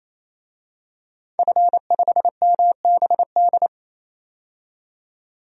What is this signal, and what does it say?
Morse code, heard as a keyed tone, about 28 wpm, 700 Hz. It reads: F5MBD
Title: DXCC-100 Call Signs - 5 Characters - 28wpm